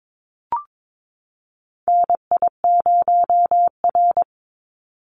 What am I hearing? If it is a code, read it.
Morse code, heard as a keyed tone, about 22 wpm, 700 Hz. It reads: NI0R
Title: Call Signs - 22wpm